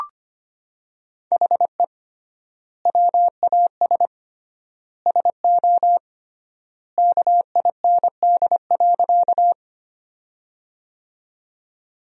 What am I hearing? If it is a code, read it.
Morse code, heard as a keyed tone, about 25 wpm, 700 Hz. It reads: HE WAS SO KIND.
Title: 3x Word-Spacing - Sentences from Top 200 Words - 25wpm